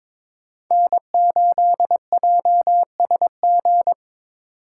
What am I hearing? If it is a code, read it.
Morse code, heard as a keyed tone, about 22 wpm, 700 Hz. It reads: N8JSG
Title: USA Call Signs - 5 Characters - 3x Character Spacing - 22wpm